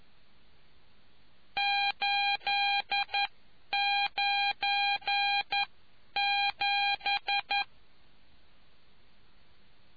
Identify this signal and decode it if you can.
Morse code, heard as a keyed tone, about 11 wpm, spaced as for 9 wpm, 810 Hz. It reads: 897